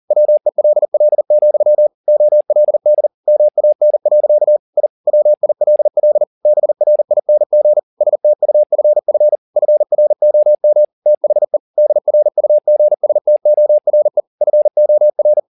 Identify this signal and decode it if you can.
Morse code, heard as a keyed tone, about 40 wpm, 600 Hz. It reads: WELL, OLD MAN. I WILL BRING STUFF FROM THE DRUGSTORE FOR
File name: the old man and the sea 2 40wpm